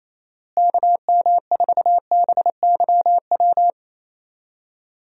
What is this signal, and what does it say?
Morse code, heard as a keyed tone, about 28 wpm, 700 Hz. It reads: KM4BYW
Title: USA Call Signs - 6 Characters - Repeated 3x - 28wpm